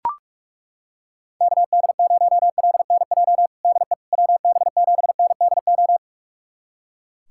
Morse code, heard as a keyed tone, about 42 wpm, 700 Hz. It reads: KD0LNJ DE WB7NDO
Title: USA Callsign DE Callsign 45wpm